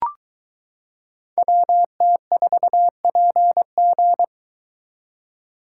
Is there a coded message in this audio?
Yes: Morse code from 1.4 to 4.2 s, heard as a keyed tone, about 22 wpm, 700 Hz. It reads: WT4PG